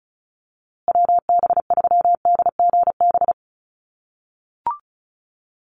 Morse code, heard as a keyed tone, about 34 wpm, 700 Hz. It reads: WB3DGB
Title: USA Call Signs Repeated 3x 35wpm